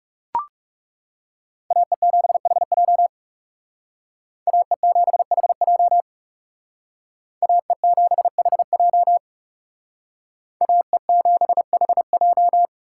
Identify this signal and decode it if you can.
Morse code, heard as a keyed tone, about 38 wpm, 700 Hz. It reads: AE7HJ AE7HJ AE7HJ AE7HJ